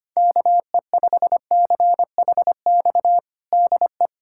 Morse code, heard as a keyed tone, about 25 wpm, 700 Hz. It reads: KE5CHX DE
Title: USA Callsign DE Callsign - 6 Characters - 25wpm